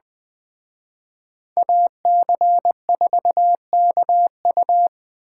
Morse code, heard as a keyed tone, about 21 wpm, 700 Hz. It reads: AC4KU